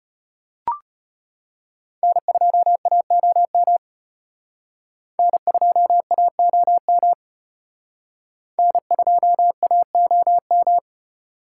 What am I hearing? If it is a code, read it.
Morse code, heard as a keyed tone, about 31 wpm, 700 Hz. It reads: N2AOM N2AOM N2AOM
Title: Speed Racing - USA Call Signs 25wpm